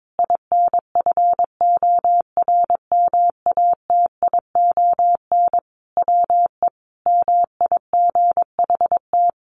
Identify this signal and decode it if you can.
Morse code, heard as a keyed tone, about 22 wpm, 700 Hz. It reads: INFORMATION WE MIGHT
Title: Sentences from Top 1300 Words 22wpm